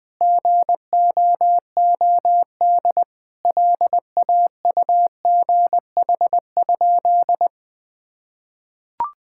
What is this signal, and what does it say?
Morse code, heard as a keyed tone, about 21 wpm, 700 Hz. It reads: GOOD LAUGH?
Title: Sentences from Top 1000 Words - Review Part 5 - 20wpm